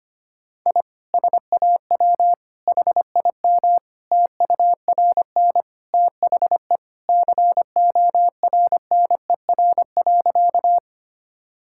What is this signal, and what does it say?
Morse code, heard as a keyed tone, about 25 wpm, 700 Hz. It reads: I SAW HIM TURN THE CORNER.
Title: Sentences from Top 1600 Words 25wpm